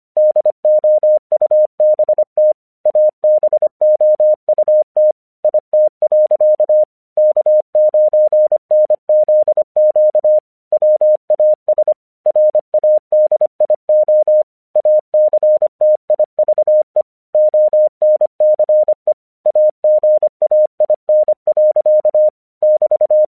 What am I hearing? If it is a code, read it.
Morse code, heard as a keyed tone, about 25 wpm, 610 Hz. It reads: DOUBT ABOUT IT. K9NZQ WAS RADIO ACTIVE ONCE AGAIN. <BT>
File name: 1 English 25wpm / AQC37